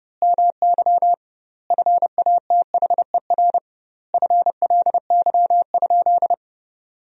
Morse code, heard as a keyed tone, about 30 wpm, 700 Hz. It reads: MY FATHER FLY?
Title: 2x Word-Spacing - Sentences from Top 500 Words - 30wpm